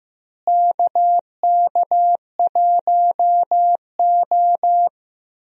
Morse code, heard as a keyed tone, about 15 wpm, 700 Hz. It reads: KK1O
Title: USA Call Signs - 4 Characters - Repeated 3x - 15wpm